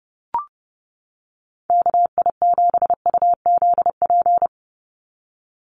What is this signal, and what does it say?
Morse code, heard as a keyed tone, about 30 wpm, 700 Hz. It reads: KI7UZP
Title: USA Call Signs 30wpm